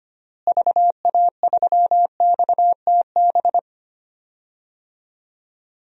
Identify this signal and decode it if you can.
Morse code, heard as a keyed tone, about 25 wpm, 700 Hz. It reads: VA3XTB
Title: DXCC-100 Call Signs - 6 Characters - Repeated 3x - 25wpm